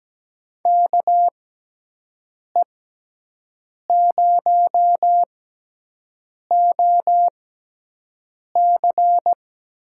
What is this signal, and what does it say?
Morse code, heard as a keyed tone, about 17 wpm, spaced as for 6 wpm, 700 Hz. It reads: KE0OC
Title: USA Call Signs - 6 Characters - 6x Character Spacing - 17wpm